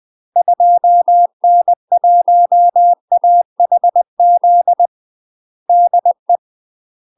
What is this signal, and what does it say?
Morse code, heard as a keyed tone, about 20 wpm, 700 Hz. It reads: 2N1AHZ DE